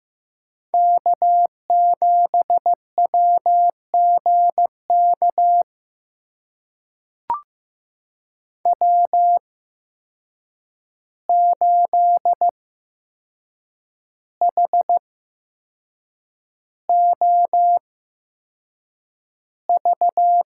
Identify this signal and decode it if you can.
Morse code, heard as a keyed tone, about 15 wpm, spaced as for 6 wpm, 700 Hz. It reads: K7WGK W8HOV
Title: USA Call Signs - 5 Characters - 8x Character Spacing - 15wpm